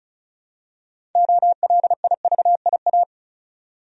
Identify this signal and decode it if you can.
Morse code, heard as a keyed tone, about 36 wpm, 700 Hz. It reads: OLIVIA
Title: USA Common Names - 35wpm